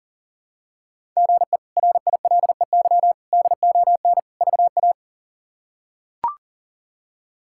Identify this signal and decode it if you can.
Morse code, heard as a keyed tone, about 40 wpm, 700 Hz. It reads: GE RILEY DON VA
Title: K1USN SST QSO Practice - 40wpm